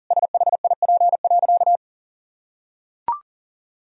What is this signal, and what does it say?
Morse code, heard as a keyed tone, about 37 wpm, 700 Hz. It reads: SHIP.
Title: Sentences from Top 1000 Words - Review Part 2 - 40wpm